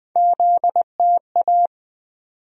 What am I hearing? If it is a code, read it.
Morse code, heard as a keyed tone, about 20 wpm, 700 Hz. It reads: ZTA
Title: USA Call Sign Postfixes - 3 Letters 20wpm